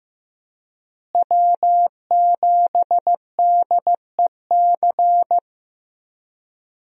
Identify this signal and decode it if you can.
Morse code, heard as a keyed tone, about 15 wpm, 700 Hz. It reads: W7DEC
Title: USA Call Signs - 5 Characters - Repeated 2x - 15wpm